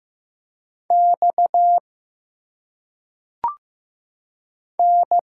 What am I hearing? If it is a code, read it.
Morse code, heard as a keyed tone, about 15 wpm, 700 Hz. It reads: X N
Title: Single Letters - Phonetic Pronunciation 15wpm